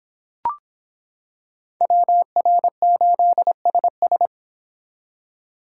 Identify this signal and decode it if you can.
Morse code, heard as a keyed tone, about 26 wpm, 700 Hz. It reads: WR8SS